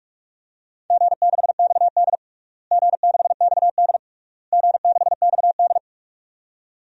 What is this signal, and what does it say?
Morse code, heard as a keyed tone, about 46 wpm, 700 Hz. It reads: G6XD G6XD G6XD
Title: DXCC-100 Call Signs - 4 Characters - Repeated 3x - 45wpm